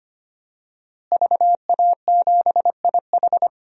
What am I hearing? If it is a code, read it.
Morse code, heard as a keyed tone, about 25 wpm, 700 Hz. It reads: VA7IH
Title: DXCC-100 Call Signs - 5 Characters - Repeated 2x - 25wpm